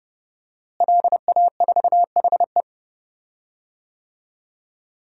Morse code, heard as a keyed tone, about 30 wpm, 700 Hz. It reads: LA4HE
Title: DXCC-100 Call Signs - 5 Characters - Repeated 3x - 30wpm